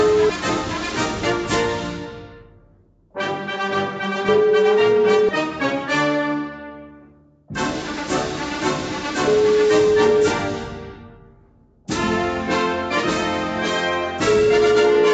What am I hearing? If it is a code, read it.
Ringback tone (Europe, 425 Hz), with noise.